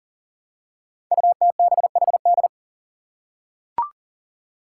Morse code, heard as a keyed tone, about 41 wpm, 700 Hz. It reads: UTBHD
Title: Five Challenging Letter-Number Combinations 40wpm